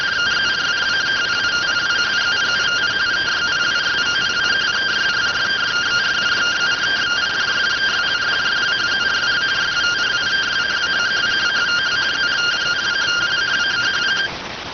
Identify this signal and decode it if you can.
Two-tone teleprinter signal, about 170 Hz shift.